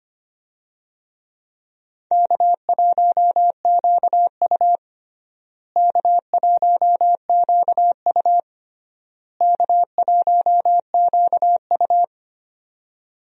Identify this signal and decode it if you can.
Morse code, heard as a keyed tone, about 25 wpm, 700 Hz. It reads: K1QU K1QU K1QU
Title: USA Call Signs - 4 Characters - Repeated 3x - 25wpm